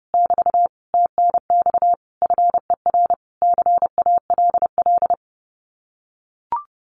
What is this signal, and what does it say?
Morse code, heard as a keyed tone, about 28 wpm, 700 Hz. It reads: <BT> TNX FER CALL
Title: Common QSO Phrases Encore 30wpm